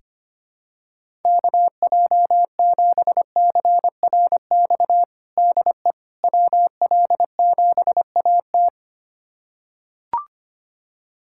Morse code, heard as a keyed tone, about 25 wpm, 700 Hz. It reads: KJ7CRX DE WL7AT
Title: USA Callsign DE Callsign 25wpm